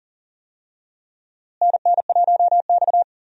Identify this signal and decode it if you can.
Morse code, heard as a keyed tone, about 40 wpm, 700 Hz. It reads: NN1X
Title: USA Call Signs - 4 Characters - Repeated 2x - 40wpm